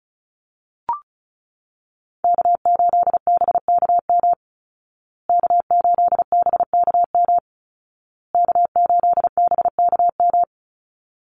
Morse code, heard as a keyed tone, about 34 wpm, 700 Hz. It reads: K8BKM K8BKM K8BKM
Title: Call Signs Repeated 2x - 35wpm